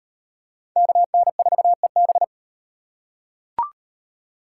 Morse code, heard as a keyed tone, about 38 wpm, 700 Hz. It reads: KN4EB